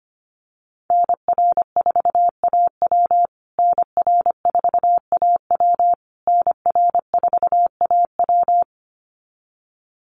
Morse code, heard as a keyed tone, about 25 wpm, 700 Hz. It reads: NR4AW NR4AW NR4AW
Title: USA Call Signs Repeated 3x 25wpm